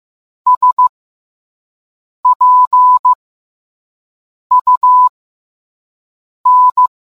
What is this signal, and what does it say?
Morse code, heard as a keyed tone, about 15 wpm, spaced as for 5 wpm, 1000 Hz. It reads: SPUN